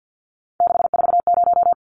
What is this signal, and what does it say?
Morse code, heard as a keyed tone, about 50 wpm, 700 Hz. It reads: 640
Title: Numbers - 3-Digit 50wpm